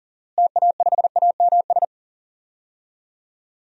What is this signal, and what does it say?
Morse code, heard as a keyed tone, about 40 wpm, 700 Hz. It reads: TA5AMS
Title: DXCC-100 Call Signs - 6 Characters - Repeated 3x - 40wpm